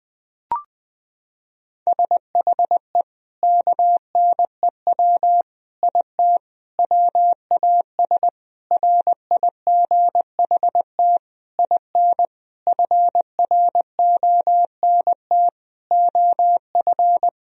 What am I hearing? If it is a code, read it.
Morse code, heard as a keyed tone, about 20 wpm, 700 Hz. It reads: SHE KNEW IT WAS RIGHT IN FRONT OF
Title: Sentences from Top 1000 Words - Review Part 2 - 20wpm